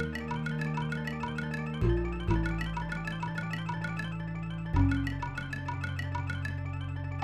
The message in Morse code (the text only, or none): none